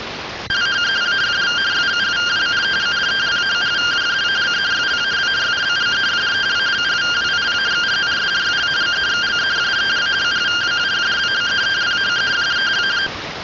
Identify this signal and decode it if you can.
Two-tone teleprinter signal, about 170 Hz shift.